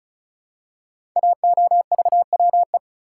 Morse code, heard as a keyed tone, about 36 wpm, 700 Hz. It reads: AOVWE
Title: Code Groups - 5 Letters - 35wpm